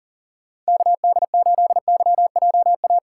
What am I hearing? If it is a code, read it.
Morse code, heard as a keyed tone, about 40 wpm, 700 Hz. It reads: KD8YJA
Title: USA Call Signs - 6 Characters - 4x Character Spacing - 40wpm